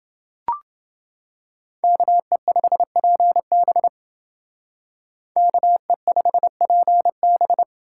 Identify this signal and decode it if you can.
Morse code, heard as a keyed tone, about 28 wpm, 700 Hz. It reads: KE5PB KE5PB